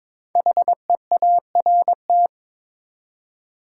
Morse code, heard as a keyed tone, about 23 wpm, 700 Hz. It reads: HEART
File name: 5 Letter Words - First 3 Letters Given - 22wpm